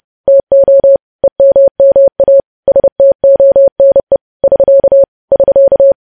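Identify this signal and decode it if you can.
Morse code, heard as a keyed tone, about 30 wpm, 550 Hz. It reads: TO EMMA STONE <SK> <SK>
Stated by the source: Morse news headlines sent at 30 wpm, opening with vvv.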